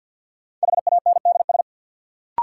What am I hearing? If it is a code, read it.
Morse code, heard as a keyed tone, about 49 wpm, 700 Hz. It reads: HANDS
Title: Sentences from Top 1000 Words - Review Part 6 - New Word Drill - 50wpm